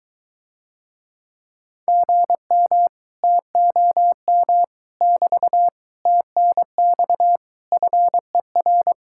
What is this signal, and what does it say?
Morse code, heard as a keyed tone, about 23 wpm, 700 Hz. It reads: GM TOM <BT> TNX FER